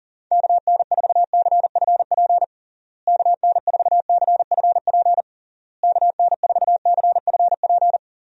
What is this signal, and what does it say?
Morse code, heard as a keyed tone, about 40 wpm, 700 Hz. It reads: KN4CFP KN4CFP KN4CFP
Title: USA Call Signs - 6 Characters - Repeated 3x - 40wpm